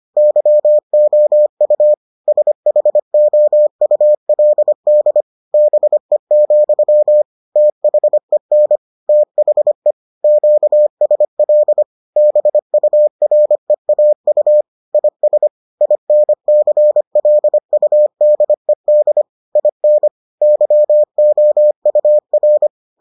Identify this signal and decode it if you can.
Morse code, heard as a keyed tone, about 25 wpm, 600 Hz. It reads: YOU SHOULD BE, THEN THE QSL BUREAU IS INCLUDED IN YOUR